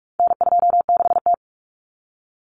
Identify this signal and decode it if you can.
Morse code, heard as a keyed tone, about 44 wpm, 700 Hz. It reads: N26T
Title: Common Cut Numbers - 4-Digit 45wpm